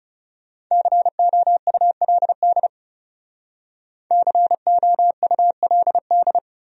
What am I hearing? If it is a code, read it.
Morse code, heard as a keyed tone, about 40 wpm, 700 Hz. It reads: COULD COULD